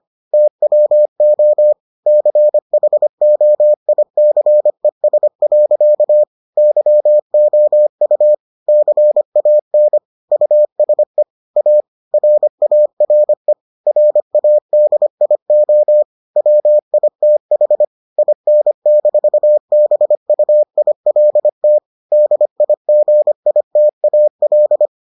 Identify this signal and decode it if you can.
Morse code, heard as a keyed tone, about 25 wpm, 600 Hz. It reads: TWO CHOICES. YOU CAN USE A RARE RADIO WITH IN-BUILT DIGITAL